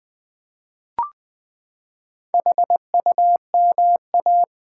Morse code, heard as a keyed tone, about 20 wpm, 700 Hz. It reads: HUMA
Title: Build A Word - 4 Letters At A Time - 12-Letter Words - 20wpm